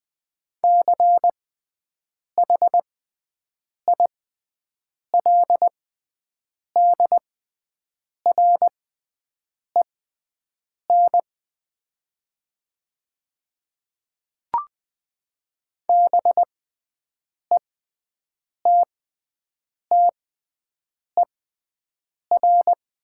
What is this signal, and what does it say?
Morse code, heard as a keyed tone, about 20 wpm, spaced as for 7 wpm, 700 Hz. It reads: CHILDREN BETTER
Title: 6x Character Spacing - Top 500 Words - 20wpm